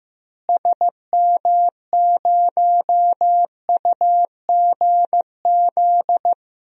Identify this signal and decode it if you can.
Morse code, heard as a keyed tone, about 15 wpm, 700 Hz. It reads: SM0UGZ
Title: DXCC-100 Call Signs - 6 Characters - Repeated 3x - 15wpm